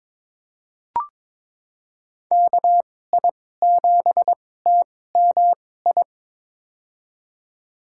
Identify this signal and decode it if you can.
Morse code, heard as a keyed tone, about 22 wpm, spaced as for 16 wpm, 700 Hz. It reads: KI7TMI
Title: USA Call Signs - 6 Characters - 2x Character Spacing - 22wpm